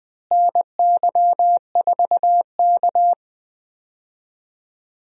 Morse code, heard as a keyed tone, about 20 wpm, 700 Hz. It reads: NY4K
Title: USA Call Signs - 4 Characters - Repeated 3x - 20wpm